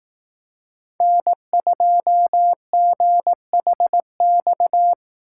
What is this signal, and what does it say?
Morse code, heard as a keyed tone, about 20 wpm, 700 Hz. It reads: N2GHX